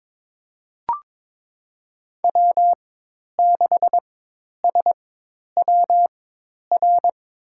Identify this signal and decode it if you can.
Morse code, heard as a keyed tone, about 22 wpm, spaced as for 10 wpm, 700 Hz. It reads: W6SWR
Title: USA Call Signs - 5 Characters - 4x Character Spacing - 22wpm